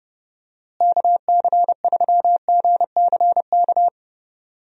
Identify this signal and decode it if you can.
Morse code, heard as a keyed tone, about 30 wpm, 700 Hz. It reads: KC3GCK
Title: USA Call Signs - 6 Characters - Repeated 3x - 30wpm